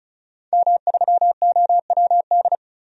Morse code, heard as a keyed tone, about 36 wpm, 700 Hz. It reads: M3OWD
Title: DXCC-100 Call Signs - 5 Characters - 35wpm